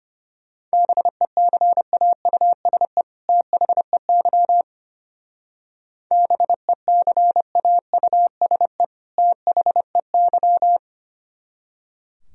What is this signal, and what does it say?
Morse code, heard as a keyed tone, about 31 wpm, 700 Hz. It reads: BECAUSE THEY BECAUSE THEY
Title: Speed-Racing - Sets of 2 Words 25wpm